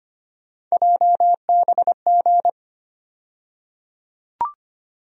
Morse code, heard as a keyed tone, about 25 wpm, 700 Hz. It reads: JBG
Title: USA Call Sign Postfixes - 3 Letters 25wpm